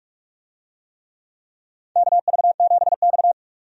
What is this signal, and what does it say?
Morse code, heard as a keyed tone, about 42 wpm, 700 Hz. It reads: KV7X
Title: USA Call Signs - 4 Characters - 45wpm